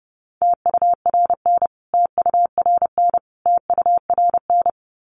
Morse code, heard as a keyed tone, about 30 wpm, 700 Hz. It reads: TURN TURN TURN
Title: Top 200 Words Repeated 30wpm